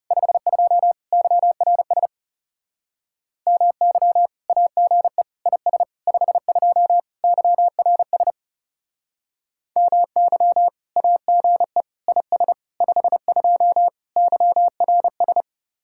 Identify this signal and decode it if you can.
Morse code, heard as a keyed tone, about 38 wpm, 700 Hz. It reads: 52 YRS MY AGE IS 52 YRS MY AGE IS 52 YRS